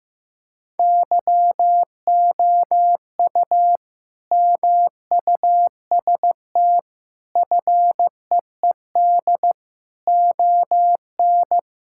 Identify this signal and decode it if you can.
Morse code, heard as a keyed tone, about 15 wpm, 700 Hz. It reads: YOU MUST FEED ON
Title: Sentences from Top 1600 Words 15wpm